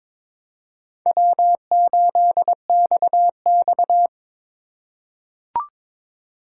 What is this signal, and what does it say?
Morse code, heard as a keyed tone, about 22 wpm, 700 Hz. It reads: W8XX
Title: USA Call Signs - 4 Characters - 4x Character Spacing - 22wpm